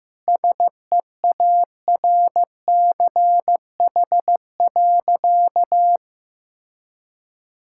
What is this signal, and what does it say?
Morse code, heard as a keyed tone, about 15 wpm, 700 Hz. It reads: SEARCH.
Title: Sentences from Top 1000 Words - Review Part 5 - 15wpm